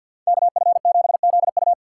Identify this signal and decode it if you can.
Morse code, heard as a keyed tone, about 53 wpm, 700 Hz. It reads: KF7ZU
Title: USA Call Signs - 5 Characters - 4x Character Spacing - 50wpm